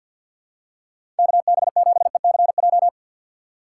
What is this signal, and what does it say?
Morse code, heard as a keyed tone, about 50 wpm, 700 Hz. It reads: KB7ECJ
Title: USA Call Signs - 6 Characters - 3x Character Spacing - 50wpm